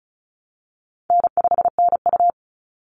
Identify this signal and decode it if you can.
Morse code, heard as a keyed tone, about 34 wpm, 700 Hz. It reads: N5NU
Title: Easy Call Signs Repeated 2x - 35wpm